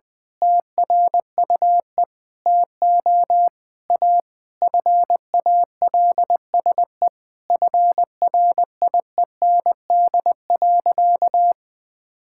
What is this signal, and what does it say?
Morse code, heard as a keyed tone, about 20 wpm, 700 Hz. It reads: TRUE TO A FALSE FRIEND.
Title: Sentences from Top 1600 Words 20wpm